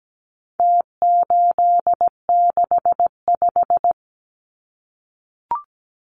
Morse code, heard as a keyed tone, about 17 wpm, 700 Hz. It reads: T865
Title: Numbers - Contest Style - 4-Digit 17wpm